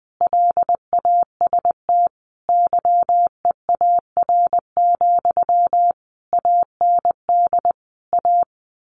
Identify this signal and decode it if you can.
Morse code, heard as a keyed tone, about 20 wpm, 700 Hz. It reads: LAST YEAR, AND A
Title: A Sound Of Thunder 20wpm